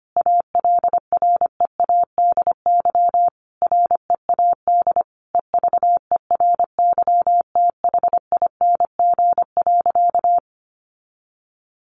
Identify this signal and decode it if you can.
Morse code, heard as a keyed tone, about 25 wpm, 700 Hz. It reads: ALREADY READ EVERYTHING.
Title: Sentences from Top 700 Words 25wpm